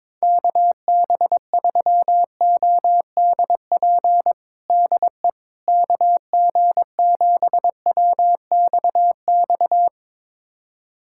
Morse code, heard as a keyed tone, about 22 wpm, 700 Hz. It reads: KB3ODP DE KG7WXX
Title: USA Callsign DE Callsign 22wpm